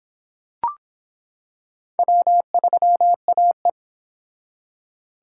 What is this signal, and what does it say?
Morse code, heard as a keyed tone, about 24 wpm, 700 Hz. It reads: W3AE